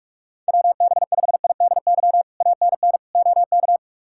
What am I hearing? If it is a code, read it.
Morse code, heard as a keyed tone, about 46 wpm, 700 Hz. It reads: WB5IDY ANN OK
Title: K1USN SST QSO Practice - 45wpm